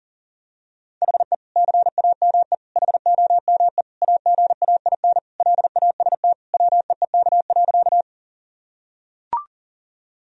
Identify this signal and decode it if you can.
Morse code, heard as a keyed tone, about 40 wpm, 700 Hz. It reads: HE CAME HOME AGAIN LAST WEEK.
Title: Sentences from Top 1000 Words - Review Part 2 - 40wpm